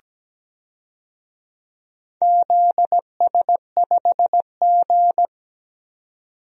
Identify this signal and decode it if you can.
Morse code, heard as a keyed tone, about 17 wpm, 700 Hz. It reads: ZS5G
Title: DXCC-100 Call Signs - 4 Characters - 17wpm